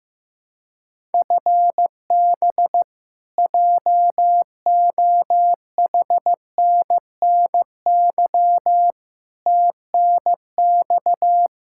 Morse code, heard as a keyed tone, about 16 wpm, 700 Hz. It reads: FB JOHNNY TNX
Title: Practice QSO - Exchange 3 - More Info - 15wpm